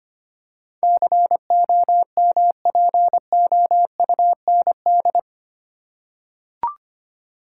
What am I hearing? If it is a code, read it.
Morse code, heard as a keyed tone, about 25 wpm, 700 Hz. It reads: COMPOUND
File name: Eight Letter Words - 25wpm